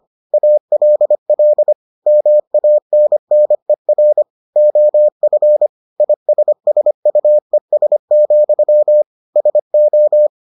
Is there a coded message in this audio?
Yes: Morse code from 0.3 to 10.3 s, heard as a keyed tone, about 25 wpm, 600 Hz. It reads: ALL MANNER OF ISSUES, SO